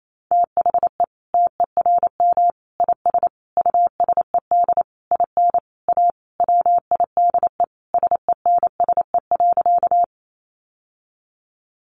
Morse code, heard as a keyed tone, about 28 wpm, 700 Hz. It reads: THE TERM IS USED IN A WIDE SENSE.
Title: Sentences from Top 900 Words 28wpm